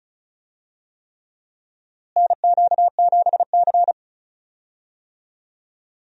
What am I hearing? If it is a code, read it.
Morse code, heard as a keyed tone, about 34 wpm, 700 Hz. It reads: NQ7C
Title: USA Call Signs - 4 Characters - Repeated 3x - 35wpm